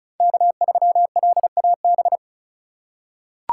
Morse code, heard as a keyed tone, about 34 wpm, 700 Hz. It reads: K3LAB
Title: USA Call Signs - 5 Characters - 2x Character Spacing - 35wpm